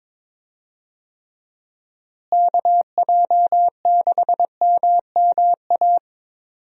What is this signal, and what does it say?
Morse code, heard as a keyed tone, about 22 wpm, 700 Hz. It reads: KJ6MMA
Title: USA Call Signs - 6 Characters - Repeated 3x - 22wpm